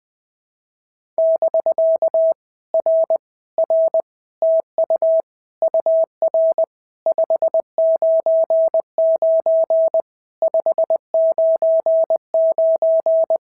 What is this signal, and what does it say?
Morse code, heard as a keyed tone, about 20 wpm, 650 Hz. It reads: <BK> R R TU UR 599 599
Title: POTA QSO Practice - 20wpm